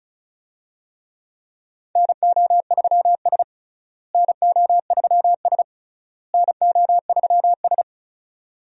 Morse code, heard as a keyed tone, about 34 wpm, 700 Hz. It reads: NO3S NO3S NO3S
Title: USA Call Signs - 4 Characters - Repeated 3x - 35wpm